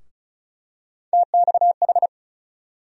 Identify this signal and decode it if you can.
Morse code, heard as a keyed tone, about 34 wpm, 700 Hz. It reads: TXH
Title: USA Call Sign Postfixes - 3 Letters 35wpm